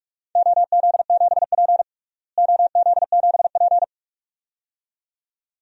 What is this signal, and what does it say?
Morse code, heard as a keyed tone, about 46 wpm, 700 Hz. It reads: OZ7P OZ7P
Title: DXCC-100 Call Signs - 4 Characters - Repeated 2x - 45wpm